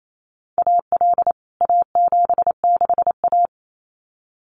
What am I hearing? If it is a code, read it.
Morse code, heard as a keyed tone, about 28 wpm, 700 Hz. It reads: AL A76A
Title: CWT Exchange 28wpm